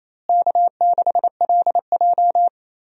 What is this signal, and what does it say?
Morse code, heard as a keyed tone, about 28 wpm, 700 Hz. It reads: K6LJ
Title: USA Call Signs - 4 Characters - Repeated 3x - 28wpm